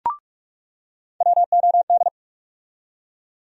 Morse code, heard as a keyed tone, about 47 wpm, 700 Hz. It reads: WOD
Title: USA Call Sign Postfixes - 3 Letters 45wpm